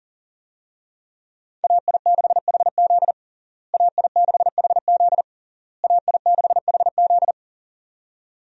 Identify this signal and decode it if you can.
Morse code, heard as a keyed tone, about 40 wpm, 700 Hz. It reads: AI6HZ AI6HZ AI6HZ
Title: USA Call Signs - 5 Characters - Repeated 3x - 40wpm